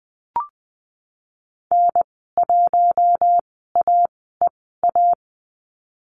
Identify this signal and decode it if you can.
Morse code, heard as a keyed tone, about 20 wpm, spaced as for 15 wpm, 700 Hz. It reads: N1AEA
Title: 042 Lesson 8g - Number 1 - Call Signs 20wpm